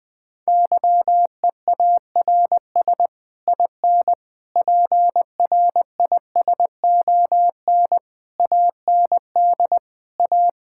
Morse code, heard as a keyed tone, about 21 wpm, 700 Hz. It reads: YEARS IN PRISON AND A
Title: Sets of 9 Words - 20wpm